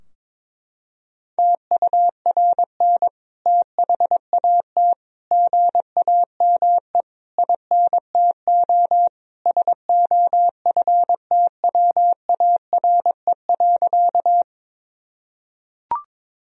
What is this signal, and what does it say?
Morse code, heard as a keyed tone, about 23 wpm, 700 Hz. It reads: TURN THAT GAME INTO SOFTWARE.
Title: Sentences from Top 1000 Words - Review Part 3 - 22wpm